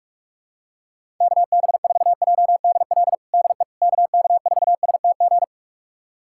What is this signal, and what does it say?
Morse code, heard as a keyed tone, about 43 wpm, 700 Hz. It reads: KB4JDL DE KK4STG
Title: USA Callsign DE Callsign 45wpm